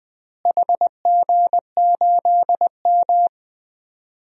Morse code, heard as a keyed tone, about 20 wpm, 700 Hz. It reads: HG8M
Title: DXCC-100 Call Signs - 4 Characters - Repeated 3x - 20wpm